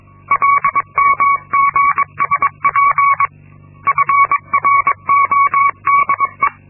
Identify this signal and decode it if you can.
Morse code, heard as a keyed tone, about 22 wpm, 1120 Hz. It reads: LMGSP FRODE
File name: Linksmakalnis-2